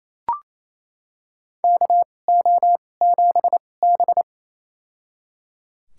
Morse code, heard as a keyed tone, about 27 wpm, spaced as for 20 wpm, 690 Hz. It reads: KO7B
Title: USA Call Signs - 4 Characters - 2x Character Spacing - 28wpm